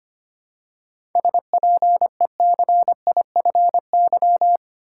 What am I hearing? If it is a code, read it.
Morse code, heard as a keyed tone, about 25 wpm, 700 Hz. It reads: SPECIFY
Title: Seven Letter Words - 25wpm